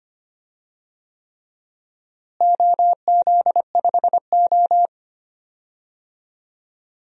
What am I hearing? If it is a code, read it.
Morse code, heard as a keyed tone, about 25 wpm, 700 Hz. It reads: OZ5O
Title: DXCC-100 Call Signs - 4 Characters - Repeated 3x - 25wpm